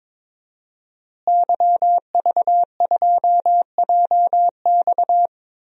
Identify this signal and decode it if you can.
Morse code, heard as a keyed tone, about 21 wpm, 700 Hz. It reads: YV2JX
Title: DXCC-100 Call Signs - 5 Characters - Repeated 3x - 22wpm